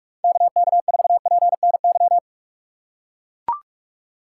Morse code, heard as a keyed tone, about 41 wpm, 700 Hz. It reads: KK4PNY
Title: USA Call Signs - 6 Characters - 2x Character Spacing - 45wpm